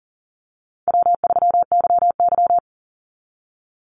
Morse code, heard as a keyed tone, about 40 wpm, 700 Hz. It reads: W3YY
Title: Easy Call Signs Repeated 3x - 40wpm